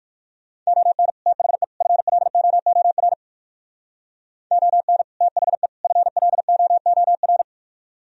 Morse code, heard as a keyed tone, about 58 wpm, 700 Hz. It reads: ON THE FLOOR ON THE FLOOR